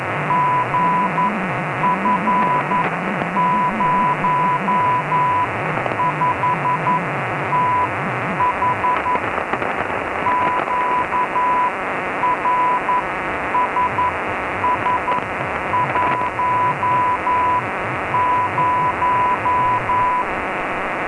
Morse code, heard as a keyed tone, about 11 wpm, spaced as for 9 wpm, 990 Hz, with radio noise. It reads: GF05TH QRSS10
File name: Notas varias / LW2ETU 137 kHz / lw2etu 30 12 03